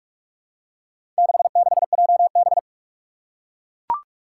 Morse code, heard as a keyed tone, about 41 wpm, 700 Hz. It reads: 66JB
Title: Four Challenging Letter-Number Combinations 45wpm